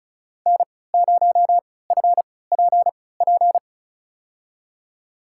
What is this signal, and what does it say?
Morse code, heard as a keyed tone, about 36 wpm, spaced as for 20 wpm, 700 Hz. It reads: N0FPP
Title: USA Call Signs - 5 Characters - 3x Character Spacing - 35wpm